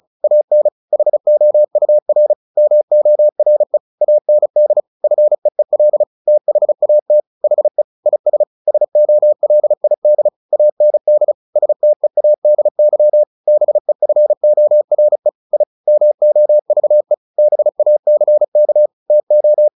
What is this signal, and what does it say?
Morse code, heard as a keyed tone, about 34 wpm, 600 Hz. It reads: AN HOUR MORE AND FEEL THAT HE IS SOLID AND STEADY BEFORE I MOVE BACK TO
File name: the old man and the sea 2 35wpm